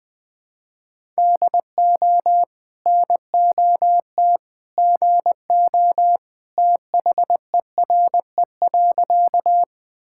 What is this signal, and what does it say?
Morse code, heard as a keyed tone, about 20 wpm, 700 Hz. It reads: DO NOT GO THERE.
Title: Sentences with 2-Word Phrases - 20wpm